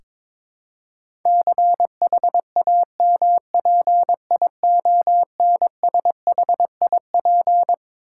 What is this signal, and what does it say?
Morse code, heard as a keyed tone, about 22 wpm, 700 Hz. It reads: CHAMPIONSHIP
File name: Twelve Letter Words - 22wpm